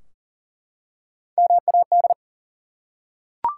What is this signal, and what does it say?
Morse code, heard as a keyed tone, about 38 wpm, 700 Hz. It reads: MAD
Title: Three Letter Words - 40wpm